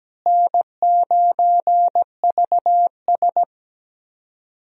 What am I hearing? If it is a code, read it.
Morse code, heard as a keyed tone, about 17 wpm, 700 Hz. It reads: N9VS
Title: USA Call Signs - 4 Characters - Repeated 2x - 17wpm